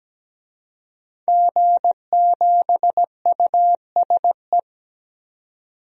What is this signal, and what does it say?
Morse code, heard as a keyed tone, about 17 wpm, 700 Hz. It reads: G7USE
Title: DXCC-100 Call Signs - 5 Characters - 17wpm